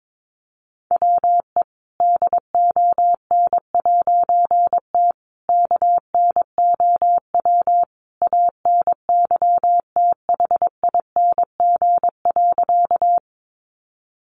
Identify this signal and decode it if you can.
Morse code, heard as a keyed tone, about 22 wpm, 700 Hz. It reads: WE DON'T KNOW ANYTHING.
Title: A Sound Of Thunder 22wpm